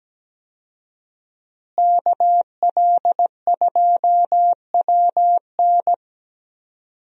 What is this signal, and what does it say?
Morse code, heard as a keyed tone, about 17 wpm, 700 Hz. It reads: KL2WN
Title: USA Call Signs - 5 Characters - Repeated 3x - 17wpm